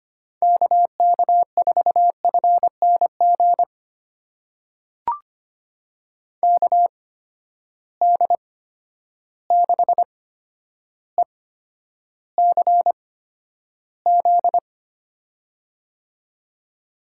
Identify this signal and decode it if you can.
Morse code, heard as a keyed tone, about 25 wpm, spaced as for 11 wpm, 700 Hz. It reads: KK4FNG KD6ECZ
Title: USA Call Signs - 6 Characters - 8x Character Spacing - 25wpm